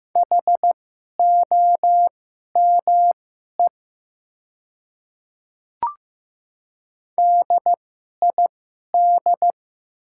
Morse code, heard as a keyed tone, about 15 wpm, spaced as for 11 wpm, 700 Hz. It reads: HOME DID
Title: 2x Character Spacing - Top 500 Words - 15wpm